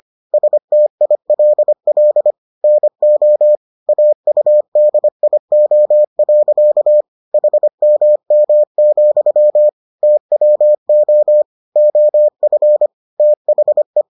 Morse code, heard as a keyed tone, about 25 wpm, 600 Hz. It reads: STILL NO AUDIO. HMM, TWO OF THE